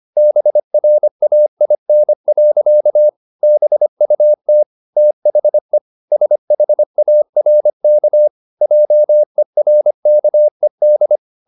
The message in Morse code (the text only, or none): BRAIN. BUT THE SHARK JERKED